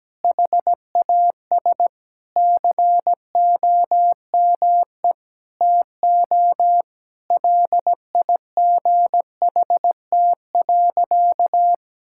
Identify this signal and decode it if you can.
Morse code, heard as a keyed tone, about 17 wpm, 700 Hz. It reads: HAS COME TO LIGHT.
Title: Sentences from Top 1000 Words - Review Part 5 - 17wpm